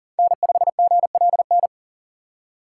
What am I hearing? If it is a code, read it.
Morse code, heard as a keyed tone, about 37 wpm, 700 Hz. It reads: N5GLN
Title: USA Call Signs - 8x Character Spacing - 40wpm